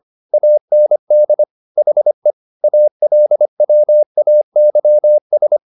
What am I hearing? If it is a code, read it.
Morse code, heard as a keyed tone, about 25 wpm, 600 Hz. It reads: AND HE ALWAYS
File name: the old man and the sea part 1 25wpm